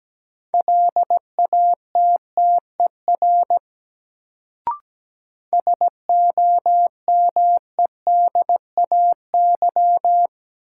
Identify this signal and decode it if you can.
Morse code, heard as a keyed tone, about 17 wpm, 700 Hz. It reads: LATTER SOMEDAY
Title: Sentences from Top 1600 Words - New Word Drill - 17wpm